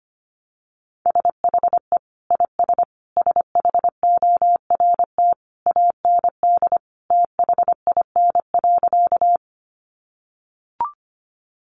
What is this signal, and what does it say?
Morse code, heard as a keyed tone, about 25 wpm, 700 Hz. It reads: SHE IS SHORT AND THIN.
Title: Sentences from Top 1200 Words 25wpm